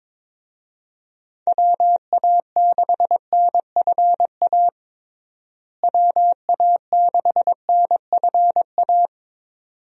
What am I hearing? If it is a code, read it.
Morse code, heard as a keyed tone, about 22 wpm, 700 Hz. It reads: WA6NFA WA6NFA
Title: USA Call Signs - 6 Characters - Repeated 2x - 22wpm